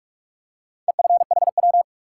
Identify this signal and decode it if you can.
Morse code, heard as a keyed tone, about 44 wpm, 700 Hz. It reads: EFHW
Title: Top 180 QSO Elements 45wpm